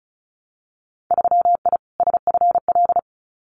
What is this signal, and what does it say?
Morse code, heard as a keyed tone, about 36 wpm, 700 Hz. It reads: 3I SFL
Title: Winter Field Day Exchanges 35wpm